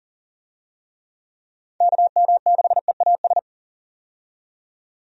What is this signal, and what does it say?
Morse code, heard as a keyed tone, about 40 wpm, 700 Hz. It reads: KM6EAS
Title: USA Call Signs - 6 Characters - 40wpm